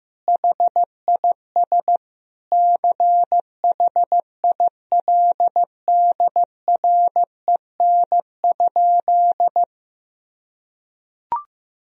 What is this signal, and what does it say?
Morse code, heard as a keyed tone, about 15 wpm, 700 Hz. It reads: HIS CHILDREN?
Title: Sentences from Top 1000 Words - Review Part 3 - 15wpm